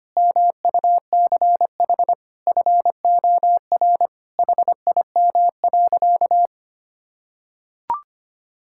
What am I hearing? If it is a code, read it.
Morse code, heard as a keyed tone, about 25 wpm, 700 Hz. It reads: MUCH FOR HIM.
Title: Sentences from Top 1600 Words 25wpm